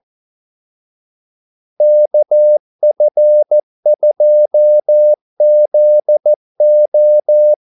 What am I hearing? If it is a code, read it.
Morse code, heard as a keyed tone, about 14 wpm, 600 Hz. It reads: KF2ZO
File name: Callsigns-14wpm